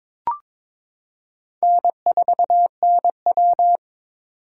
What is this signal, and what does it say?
Morse code, heard as a keyed tone, about 22 wpm, 700 Hz. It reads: N4NW
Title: K1USN SST QSO Practice - 22wpm